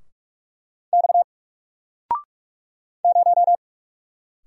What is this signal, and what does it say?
Morse code, heard as a keyed tone, about 44 wpm, 700 Hz. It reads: X 0
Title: Single Letter-Number 45wpm